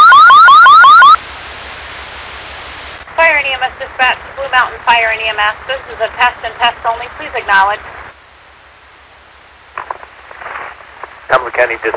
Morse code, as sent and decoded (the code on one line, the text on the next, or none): none